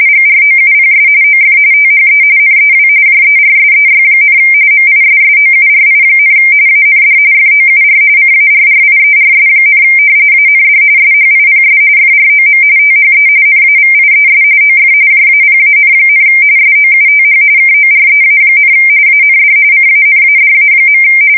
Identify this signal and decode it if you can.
RTTY, about 45.45 baud. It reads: GROUPS OF THE NEW JERSEY, MISSOURI, AND FOUR CARRIERS, AND SHIPS FROM THE NAVIES OF JAPAN, SOUTH KOREA, AND CANADA.
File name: fsk170 / battleshipnj